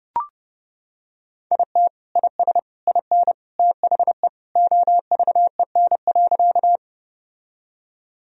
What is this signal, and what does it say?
Morse code, heard as a keyed tone, about 30 wpm, 700 Hz. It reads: IT IS IN THE OVEN.
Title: Sentences from Top 1000 Words - Review Part 4 - 30wpm